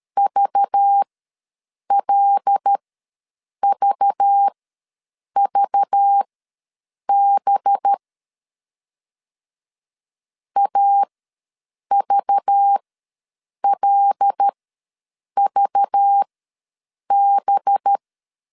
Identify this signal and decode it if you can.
Morse code, heard as a keyed tone, about 13 wpm, spaced as for 7 wpm, 800 Hz. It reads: VLVVB AVLVB
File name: LEZIONI / 5 LBVC / 5 LBV ws 12wpm 3015